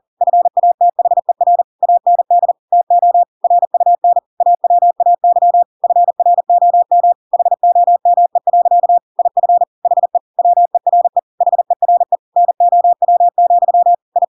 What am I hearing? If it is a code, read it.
Morse code, heard as a keyed tone, about 40 wpm, 700 Hz. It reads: FATHER AND TO RUN AWAY FROM HOME. IF HE WERE HERE NOW, I
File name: Pinocchio / Pinocchio ch5 40 40 ik1hge